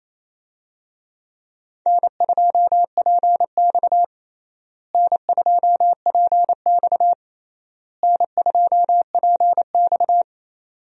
Morse code, heard as a keyed tone, about 27 wpm, 700 Hz. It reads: N2PX N2PX N2PX
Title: USA Call Signs - 4 Characters - Repeated 3x - 28wpm